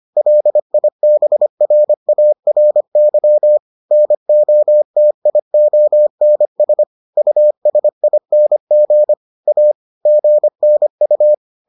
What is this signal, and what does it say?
Morse code, heard as a keyed tone, about 25 wpm, 600 Hz. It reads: LIBRARY NOTIONS USING A GNU